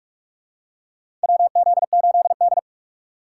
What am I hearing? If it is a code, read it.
Morse code, heard as a keyed tone, about 42 wpm, 700 Hz. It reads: WZ8D
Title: USA Call Signs - 8x Character Spacing - 45wpm